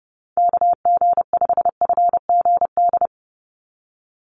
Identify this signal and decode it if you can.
Morse code, heard as a keyed tone, about 28 wpm, 700 Hz. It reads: KG5FGD
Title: USA Call Signs 30wpm